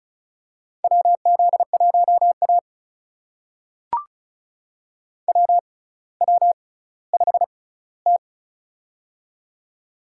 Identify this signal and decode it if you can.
Morse code, heard as a keyed tone, about 36 wpm, spaced as for 18 wpm, 700 Hz. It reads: WZ1A WW5T
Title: USA Call Signs - 6x Character Spacing - 35wpm